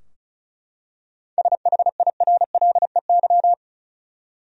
Morse code, heard as a keyed tone, about 34 wpm, 700 Hz. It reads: SHIRLEY
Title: USA Common Names - 35wpm